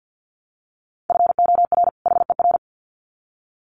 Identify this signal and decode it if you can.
Morse code, heard as a keyed tone, about 48 wpm, 700 Hz. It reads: FOR HER
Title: Sets of 2 Words 50wpm